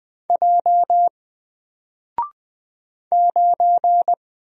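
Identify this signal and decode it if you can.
Morse code, heard as a keyed tone, about 20 wpm, 700 Hz. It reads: J 9
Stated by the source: Single letter and number Morse drill at 20 wpm, from N4LCD.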